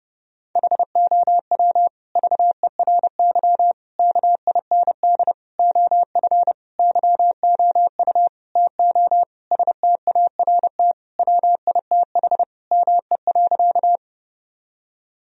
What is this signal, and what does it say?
Morse code, heard as a keyed tone, about 30 wpm, 700 Hz. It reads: HOW VERY KIND OF YOU TO START WITH ME.
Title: Sentences from Top 1000 Words - Review Part 1 - 30wpm